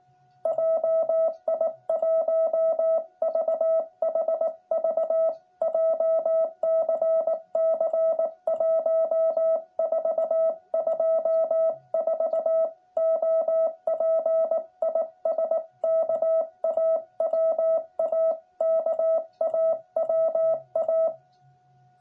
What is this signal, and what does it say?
Morse code, heard as a keyed tone, about 19 wpm, 650 Hz. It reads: JI1VHV JCC 1424 OP IS KAWA KAWA
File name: jh8cbh / 2021cw